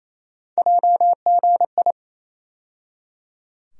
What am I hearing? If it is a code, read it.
Morse code, heard as a keyed tone, about 28 wpm, 700 Hz. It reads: JGI
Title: USA Call Sign Postfixes - 3 Letters 28wpm